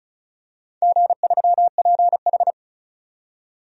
Morse code, heard as a keyed tone, about 36 wpm, 700 Hz. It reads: G3PH
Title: DXCC-100 Call Signs - 4 Characters - 35wpm